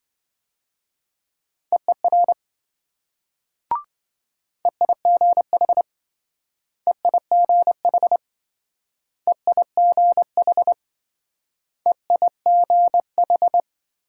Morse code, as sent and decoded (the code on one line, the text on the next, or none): . .-. / . .. --. .... / . .. --. .... / . .. --. .... / . .. --. ....
EER EIGH EIGH EIGH EIGH